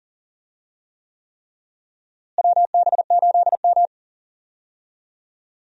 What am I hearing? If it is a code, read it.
Morse code, heard as a keyed tone, about 40 wpm, 700 Hz. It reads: WB8M
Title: USA Call Signs - 4 Characters - Repeated 3x - 40wpm